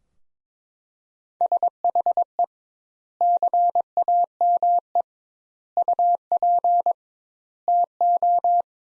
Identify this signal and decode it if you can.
Morse code, heard as a keyed tone, about 22 wpm, 700 Hz. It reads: SHE CAME UP TO
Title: Sentences Top 200 Words - 22wpm